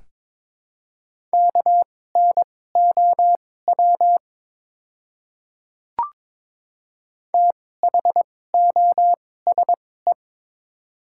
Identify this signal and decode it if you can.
Morse code, heard as a keyed tone, about 22 wpm, spaced as for 16 wpm, 700 Hz. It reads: KNOW THOSE
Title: 2x Character Spacing - Top 500 Words - 22wpm